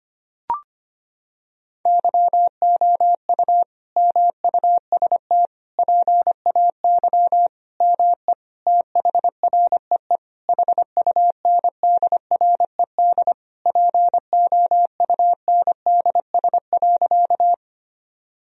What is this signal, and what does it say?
Morse code, heard as a keyed tone, about 25 wpm, 700 Hz. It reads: YOU MUST PAY ME THREE HUNDRED POUNDS.
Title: Sentences from Top 1000 Words - Review Part 2 - 25wpm